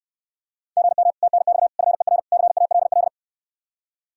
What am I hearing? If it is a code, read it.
Morse code, heard as a keyed tone, about 67 wpm, 700 Hz. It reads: GM TNX FER CALL